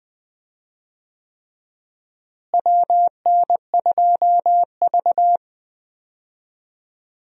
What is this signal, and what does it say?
Morse code, heard as a keyed tone, about 20 wpm, 700 Hz. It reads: WN2V
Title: USA Call Signs - 4 Characters - Repeated 3x - 20wpm